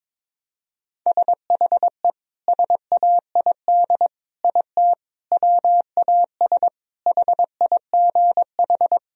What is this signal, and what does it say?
Morse code, heard as a keyed tone, about 22 wpm, 700 Hz. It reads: SHE SAID IT WAS HIGH
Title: Sentences from Top 1000 Words - Review Part 1 - 22wpm